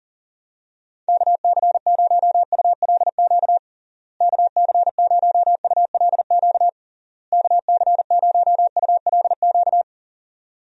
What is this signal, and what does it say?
Morse code, heard as a keyed tone, about 40 wpm, 700 Hz. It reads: KC0ULQ KC0ULQ KC0ULQ
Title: USA Call Signs - 6 Characters - Repeated 3x - 40wpm